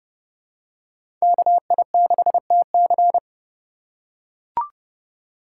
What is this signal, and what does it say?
Morse code, heard as a keyed tone, about 30 wpm, 700 Hz. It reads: KI6TC
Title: USA Call Signs - 2x Character Spacing - 30wpm